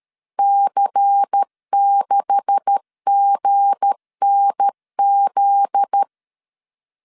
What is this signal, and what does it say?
Morse code, heard as a keyed tone, about 13 wpm, 800 Hz. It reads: C6GNZ
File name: LEZIONI / 10 SIGNS / 10A SLASH 12wpm 2314